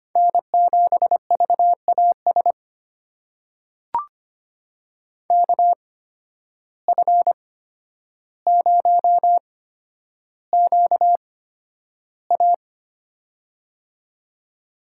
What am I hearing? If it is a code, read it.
Morse code, heard as a keyed tone, about 25 wpm, spaced as for 11 wpm, 700 Hz. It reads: N7VAS KF0QA
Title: USA Call Signs - 5 Characters - 8x Character Spacing - 25wpm